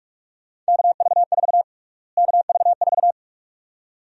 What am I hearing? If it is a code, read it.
Morse code, heard as a keyed tone, about 45 wpm, 700 Hz. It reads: KV4 KV4
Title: USA Call Sign Prefixes - Repeated 2x - 45wpm